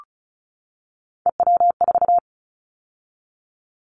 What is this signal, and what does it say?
Morse code, heard as a keyed tone, about 34 wpm, 700 Hz. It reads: EW4
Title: Callsign Prefixes for Top 100 DXCC - 35wpm